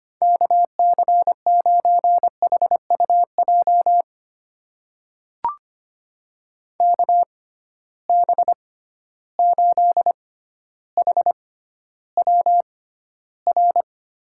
Morse code, heard as a keyed tone, about 25 wpm, spaced as for 13 wpm, 700 Hz. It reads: KC9HUJ KB8HWR
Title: USA Call Signs - 6 Characters - 6x Character Spacing - 25wpm